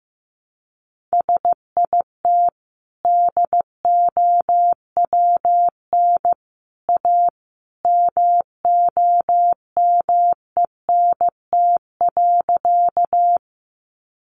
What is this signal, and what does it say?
Morse code, heard as a keyed tone, about 15 wpm, 700 Hz. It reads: SIT DOWN A MOMENT.
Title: Sentences from Top 1000 Words 15wpm